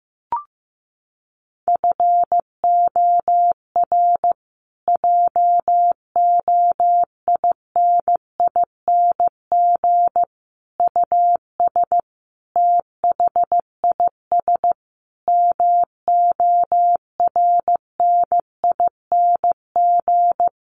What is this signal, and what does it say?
Morse code, heard as a keyed tone, about 15 wpm, 700 Hz. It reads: FOR JOINING US THIS MORNING
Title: Sets of 5 Words Encore 15wpm